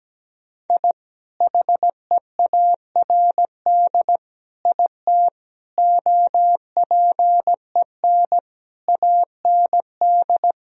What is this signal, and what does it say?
Morse code, heard as a keyed tone, about 17 wpm, 700 Hz. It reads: I HEARD IT OPEN AND
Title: Sentences from Top 1000 Words - Review Part 1 - 17wpm